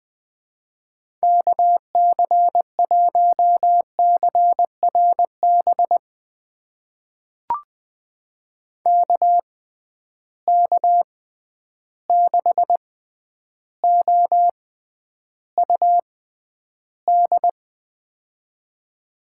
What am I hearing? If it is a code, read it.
Morse code, heard as a keyed tone, about 20 wpm, spaced as for 10 wpm, 700 Hz. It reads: KC1CRB KK6OUD
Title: USA Call Signs - 6 Characters - 6x Character Spacing - 20wpm